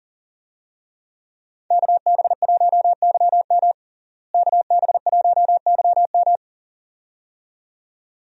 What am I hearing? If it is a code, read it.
Morse code, heard as a keyed tone, about 40 wpm, 700 Hz. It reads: KB1YM KB1YM
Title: USA Call Signs - 5 Characters - Repeated 2x - 40wpm